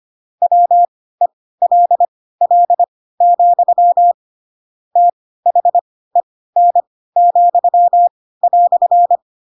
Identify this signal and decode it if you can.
Morse code, heard as a keyed tone, about 25 wpm, spaced as for 16 wpm, 700 Hz. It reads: WELL, THEN,"
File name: Pinocchio / Pinocchio ch7 10 25 ik1hge